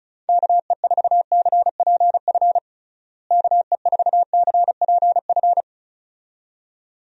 Morse code, heard as a keyed tone, about 36 wpm, 700 Hz. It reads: KE4CPF KE4CPF
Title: USA Call Signs - 6 Characters - Repeated 2x - 35wpm